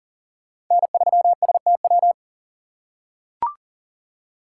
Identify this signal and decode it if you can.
Morse code, heard as a keyed tone, about 37 wpm, 700 Hz. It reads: N3STW
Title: USA Call Signs - 5 Characters - 4x Character Spacing - 40wpm